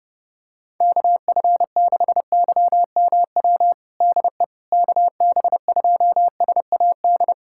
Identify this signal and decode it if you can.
Morse code, heard as a keyed tone, about 30 wpm, 700 Hz. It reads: KF6YMW DE KB2SAD
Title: USA Callsign DE Callsign 30wpm